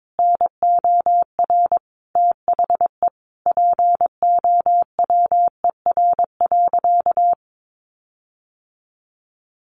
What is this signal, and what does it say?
Morse code, heard as a keyed tone, about 22 wpm, 700 Hz. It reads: NOR THE POWER.
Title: Sentences from Top 1500 Words 22wpm